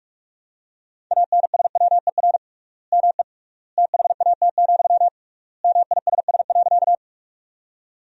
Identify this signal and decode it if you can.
Morse code, heard as a keyed tone, about 42 wpm, 700 Hz. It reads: ANSWER ME THAT, MISS.
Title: 3x Word-Spacing - Sentences from Top 500 Words - 45wpm